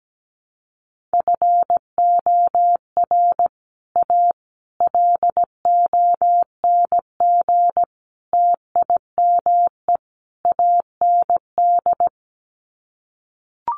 Morse code, heard as a keyed tone, about 17 wpm, 700 Hz. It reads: FOR A LONG TIME AND
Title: Sets of 5 Words Encore2 17wpm